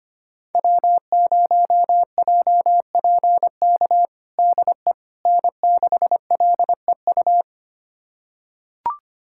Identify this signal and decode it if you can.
Morse code, heard as a keyed tone, about 25 wpm, 700 Hz. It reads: W0JPK DE N6LEU
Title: USA Callsign DE Callsign - 5 Characters - 25wpm